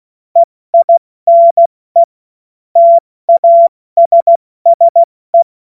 Morse code, heard as a keyed tone, about 16 wpm, spaced as for 14 wpm, 680 Hz. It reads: EINE TASSE
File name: Morsekurs / Witze / Witz 25. Mai 2021, 70 F80